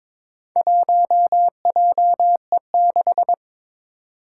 Morse code, heard as a keyed tone, about 22 wpm, 700 Hz. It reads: 1JE6
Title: Four Challenging Letter-Number Combinations 22wpm